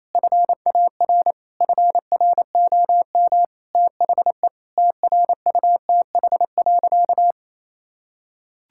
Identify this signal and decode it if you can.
Morse code, heard as a keyed tone, about 28 wpm, 700 Hz. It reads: FAR FROM THE TRUTH.
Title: Sentences from Top 1600 Words 28wpm